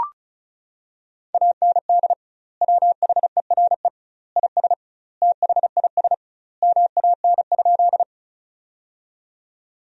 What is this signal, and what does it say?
Morse code, heard as a keyed tone, about 34 wpm, 700 Hz. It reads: AND WHERE IS THIS MAN?
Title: 2x Word-Spacing - Sentences from Top 200 Words - 35wpm